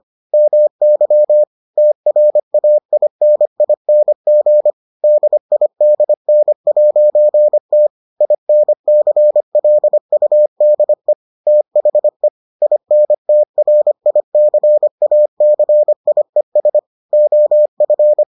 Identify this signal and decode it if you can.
Morse code, heard as a keyed tone, about 25 wpm, 600 Hz. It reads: MY TRAINING DIDN'T INCLUDE THE INTRICACIES OF